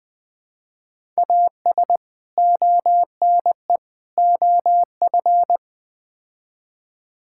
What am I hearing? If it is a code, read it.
Morse code, heard as a keyed tone, about 20 wpm, 700 Hz. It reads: AS ONE OF
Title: Build A Phrase - 4-Word Phrases - 20wpm